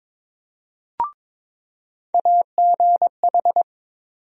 Morse code, heard as a keyed tone, about 21 wpm, 700 Hz. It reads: AGH
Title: USA Call Sign Postfixes - 3 Letters 22wpm